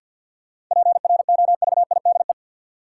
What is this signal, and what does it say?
Morse code, heard as a keyed tone, about 50 wpm, 700 Hz. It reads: PROVIDE
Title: Sentences from Top 1000 Words - Review Part 2 - New Word Drill - 50wpm